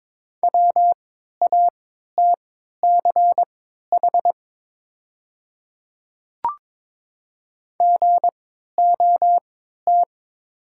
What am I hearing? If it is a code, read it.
Morse code, heard as a keyed tone, about 22 wpm, spaced as for 12 wpm, 690 Hz. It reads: WATCH GOT
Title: 3x Character Spacing - Top 300 Words - 22wpm